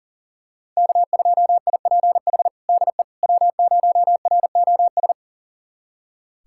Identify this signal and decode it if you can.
Morse code, heard as a keyed tone, about 40 wpm, 690 Hz. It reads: K2IPH DE W0ROS
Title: USA Callsign DE Callsign - 5 Characters - 40wpm